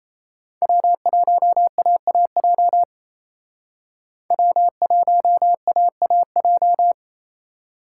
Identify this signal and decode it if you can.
Morse code, heard as a keyed tone, about 36 wpm, 700 Hz. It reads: W1AAJ W1AAJ